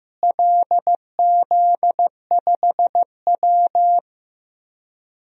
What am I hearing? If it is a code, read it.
Morse code, heard as a keyed tone, about 16 wpm, 700 Hz. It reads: LZ5W
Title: DXCC-100 Call Signs - 4 Characters - Repeated 3x - 15wpm